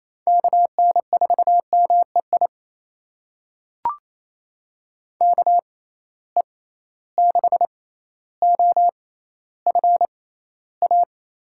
Morse code, heard as a keyed tone, about 28 wpm, spaced as for 14 wpm, 700 Hz. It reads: KN4MEI KE6OFA
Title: USA Call Signs - 6 Characters - 6x Character Spacing - 28wpm